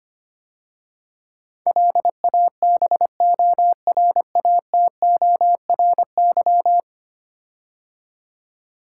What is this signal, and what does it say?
Morse code, heard as a keyed tone, about 25 wpm, 700 Hz. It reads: LABORATORY
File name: 10 Letter Words - First 3 Letters Given - 25wpm